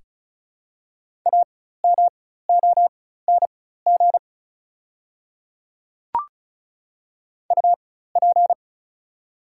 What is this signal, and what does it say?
Morse code, heard as a keyed tone, about 34 wpm, spaced as for 16 wpm, 700 Hz. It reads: AMONG UP
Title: 4x Character Spacing - Top 500 Words - 35wpm